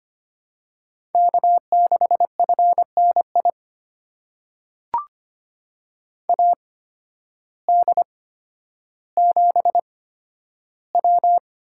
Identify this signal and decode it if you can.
Morse code, heard as a keyed tone, about 25 wpm, spaced as for 12 wpm, 700 Hz. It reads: K6FNI AD7W
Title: USA Call Signs - 8x Character Spacing - 25wpm